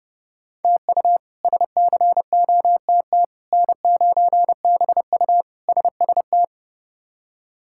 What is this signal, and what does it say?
Morse code, heard as a keyed tone, about 32 wpm, 700 Hz. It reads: TU SCOTT N9BU SST
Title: K1USN SST QSO Practice - 30wpm